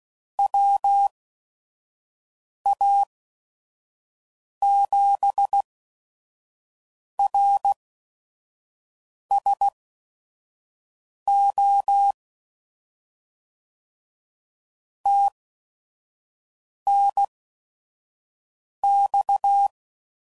Morse code, heard as a keyed tone, about 16 wpm, spaced as for 5 wpm, 800 Hz. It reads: WA7RSO TNX
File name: CW Code Groups @5WPM / QSOs / kc7sac